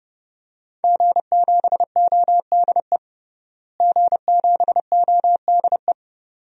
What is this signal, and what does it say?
Morse code, heard as a keyed tone, about 30 wpm, 700 Hz. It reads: G7ODE G7ODE
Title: DXCC-100 Call Signs - 5 Characters - Repeated 2x - 30wpm